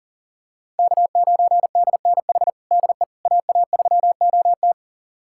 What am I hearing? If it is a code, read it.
Morse code, heard as a keyed tone, about 40 wpm, 690 Hz. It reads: K9DNH DE AA3OT
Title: USA Callsign DE Callsign - 5 Characters - 40wpm